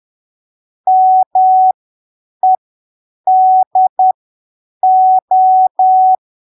Morse code, heard as a keyed tone, about 10 wpm, spaced as for 7 wpm, 750 Hz. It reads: MEDO
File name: arquivos / texto / texto 19 10ppm